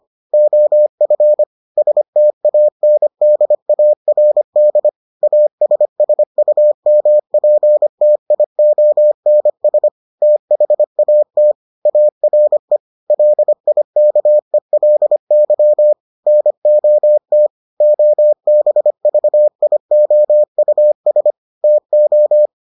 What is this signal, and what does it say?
Morse code, heard as a keyed tone, about 25 wpm, 600 Hz. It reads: OF STANDARD ASSUMPTIONS THAT ARE LIKELY NOT OBVIOUS TO